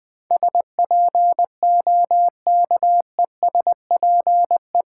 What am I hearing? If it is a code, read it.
Morse code, heard as a keyed tone, about 20 wpm, 700 Hz. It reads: SPOKESPE
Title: Build A Word - 4 Letters At A Time - 12-Letter Words - 20wpm